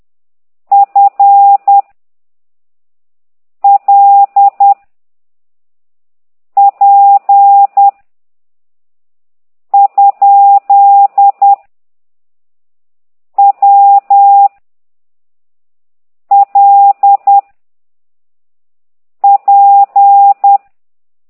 Morse code, heard as a keyed tone, about 10 wpm, 800 Hz. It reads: F L P ? W L P
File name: ct2idw / exerc9